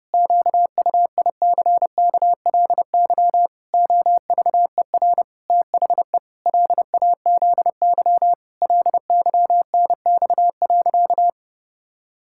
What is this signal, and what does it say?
Morse code, heard as a keyed tone, about 30 wpm, 700 Hz. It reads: QUICKLY OVER THE LAZY LYNX.
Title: Pangram Challenge v2 - 30wpm